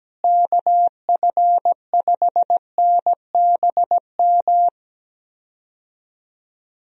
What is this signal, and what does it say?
Morse code, heard as a keyed tone, about 17 wpm, 700 Hz. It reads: KF5NBM
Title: USA Call Signs - 6 Characters - Repeated 2x - 17wpm